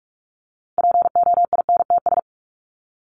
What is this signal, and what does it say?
Morse code, heard as a keyed tone, about 44 wpm, 700 Hz. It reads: POINTS
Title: ES:S Suffix - Top 100 Words with Base Words 45wpm